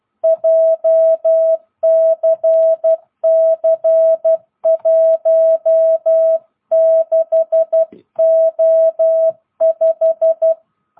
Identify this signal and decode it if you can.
Morse code, heard as a keyed tone, about 12 wpm, 660 Hz. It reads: JCC16O5
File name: jh8cbh / 2022cw